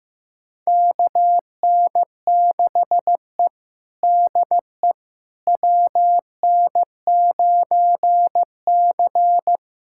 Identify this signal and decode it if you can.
Morse code, heard as a keyed tone, about 16 wpm, 700 Hz. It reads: KN6E DE WN9C
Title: USA Callsign DE Callsign - 4 Characters - 15wpm